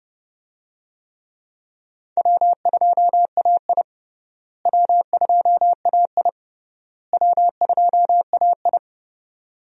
Morse code, heard as a keyed tone, about 30 wpm, 700 Hz. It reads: W2AI W2AI W2AI
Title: USA Call Signs - 4 Characters - Repeated 3x - 30wpm